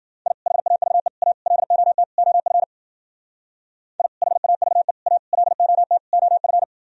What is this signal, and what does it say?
Morse code, heard as a keyed tone, about 52 wpm, 700 Hz. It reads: I HAVE A LOT OF I HAVE A LOT OF